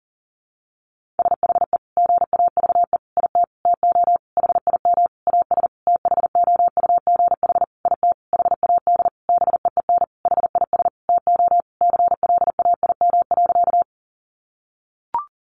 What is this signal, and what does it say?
Morse code, heard as a keyed tone, about 40 wpm, 700 Hz. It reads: SHE GAVE IT TO HIM AS THOUGH IT HAD BEEN HIS TO CLAIM.
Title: Sentences from Top 1500 Words 40wpm